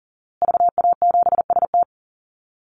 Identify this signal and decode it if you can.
Morse code, heard as a keyed tone, about 40 wpm, 700 Hz. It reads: VA7ST
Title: Call Signs Repeated 3x - 40wpm